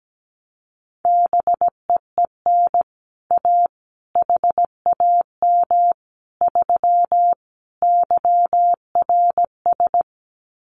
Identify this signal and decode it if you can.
Morse code, heard as a keyed tone, about 17 wpm, 700 Hz. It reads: BEEN A HAM 3 YRS
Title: Common QSO Phrases Encore 17wpm